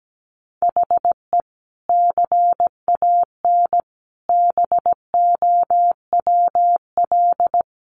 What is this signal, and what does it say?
Morse code, heard as a keyed tone, about 17 wpm, 700 Hz. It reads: HE CAN BOWL
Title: Sentences from Top 1500 Words 17wpm